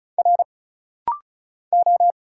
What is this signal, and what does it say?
Morse code, heard as a keyed tone, about 36 wpm, 700 Hz. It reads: R O
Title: Single Letters - Rapid-Fire 35wpm